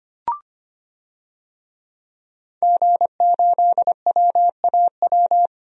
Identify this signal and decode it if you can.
Morse code, heard as a keyed tone, about 25 wpm, 700 Hz. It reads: G8WAW
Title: DXCC-100 Call Signs - 5 Characters - Repeated 3x - 25wpm